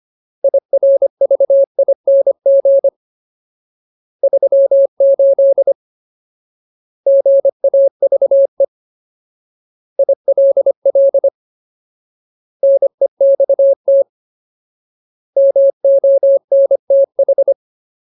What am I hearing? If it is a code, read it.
Morse code, heard as a keyed tone, about 25 wpm, 550 Hz. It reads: IRVING 38 GAVE ILL NEXT MONTH
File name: Ultimate Word List / 25WPM Ultimate